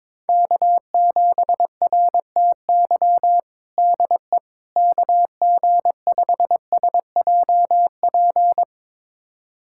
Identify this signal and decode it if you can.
Morse code, heard as a keyed tone, about 22 wpm, 700 Hz. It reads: K7RTY DE KG5SJP
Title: USA Callsign DE Callsign 22wpm